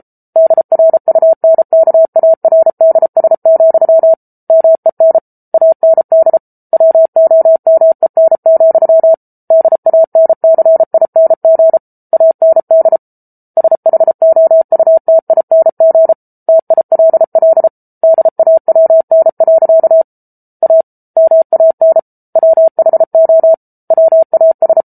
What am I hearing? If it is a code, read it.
Morse code, heard as a keyed tone, about 34 wpm, 650 Hz. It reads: DRUNKARDS, MEN AND WOMEN, DANCING AND SHOUTING TILL DAWN. A MAN WHO WAS